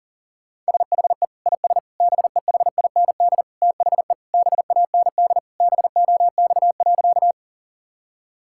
Morse code, heard as a keyed tone, about 40 wpm, 700 Hz. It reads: SHE IS BEHIND THE BAND BOX.
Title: Sentences from Top 1600 Words 40wpm